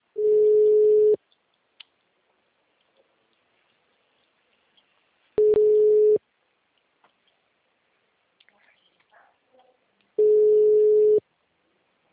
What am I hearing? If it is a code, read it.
Ringback tone (Europe, 425 Hz).
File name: out- -2009- - - .92062